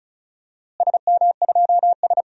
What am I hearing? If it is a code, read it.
Morse code, heard as a keyed tone, about 36 wpm, 700 Hz. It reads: SM2S
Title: DXCC-100 Call Signs - 4 Characters - Repeated 3x - 35wpm